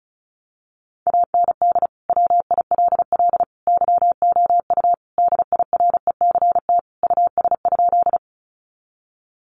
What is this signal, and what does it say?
Morse code, heard as a keyed tone, about 34 wpm, 700 Hz. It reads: AND WILL YOU DIRECT US?
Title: Sentences from Top 400 Words 35wpm